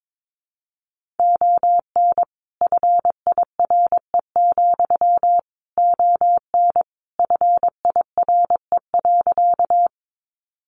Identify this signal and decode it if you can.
Morse code, heard as a keyed tone, about 22 wpm, 700 Hz. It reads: ON FIRE, ON FIRE.
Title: Sentences from Top 400 Words 22wpm